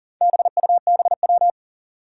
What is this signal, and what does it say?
Morse code, heard as a keyed tone, about 37 wpm, 700 Hz. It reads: BUBW
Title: Four Challenging Letter-Number Combinations 40wpm